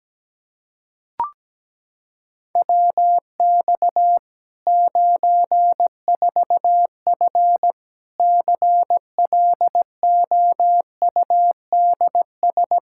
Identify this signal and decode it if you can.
Morse code, heard as a keyed tone, about 17 wpm, 700 Hz. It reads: WX 94F CLOUDS
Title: Practice QSO - Exchange 2 - Info - 17wpm